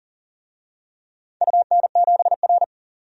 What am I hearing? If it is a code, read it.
Morse code, heard as a keyed tone, about 39 wpm, 700 Hz. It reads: UN7R
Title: DXCC-100 Call Signs - 4 Characters - 40wpm